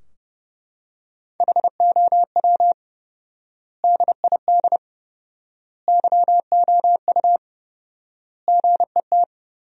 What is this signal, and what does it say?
Morse code, heard as a keyed tone, about 30 wpm, 700 Hz. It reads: HOW DID YOU GET
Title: 4x Word-Spacing - Sentences from Top 200 Words - 30wpm